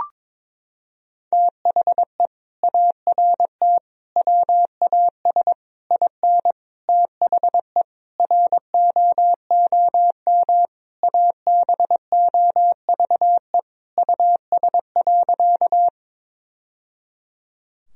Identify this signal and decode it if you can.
Morse code, heard as a keyed tone, about 22 wpm, 700 Hz. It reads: THE ART WAS IN THE ROOM ABOVE US.
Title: Sentences from Top 1000 Words - Review Part 3 - 22wpm